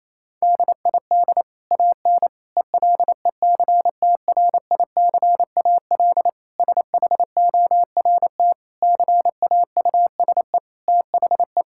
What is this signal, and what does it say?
Morse code, heard as a keyed tone, about 28 wpm, 690 Hz. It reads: DID AN ELECTRICAL SHORT CAUSE THE
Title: Sentences from Top 1000 Words - Review Part 6 - 28wpm